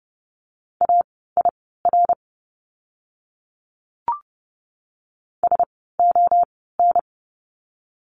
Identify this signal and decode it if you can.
Morse code, heard as a keyed tone, about 30 wpm, spaced as for 17 wpm, 700 Hz. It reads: AIR SON
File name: Lesson 17f - Letter u - Words 30wpm